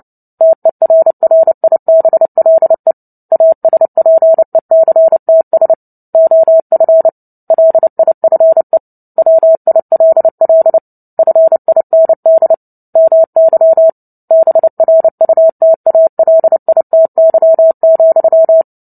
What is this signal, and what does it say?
Morse code, heard as a keyed tone, about 30 wpm, 650 Hz. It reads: TERRIBLE ASPECTS OF LIFE WILL FIND MY BRUTALITY,
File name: Book Two / 03-Chapter 3 - 29 WPM